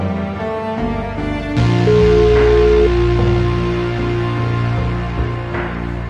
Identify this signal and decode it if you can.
Ringback tone (Europe, 425 Hz), with noise.